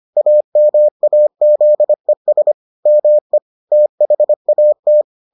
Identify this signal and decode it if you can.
Morse code, heard as a keyed tone, about 25 wpm, 600 Hz. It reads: AMAZES ME THAT